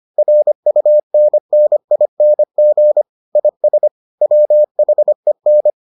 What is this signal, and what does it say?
Morse code, heard as a keyed tone, about 25 wpm, 600 Hz. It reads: RUNNING IS WHEN